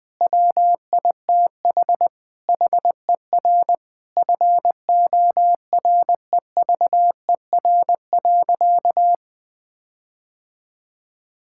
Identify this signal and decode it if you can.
Morse code, heard as a keyed tone, about 21 wpm, 700 Hz. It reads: WITH HER FOREVER.
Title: Sentences from Top 1600 Words 20wpm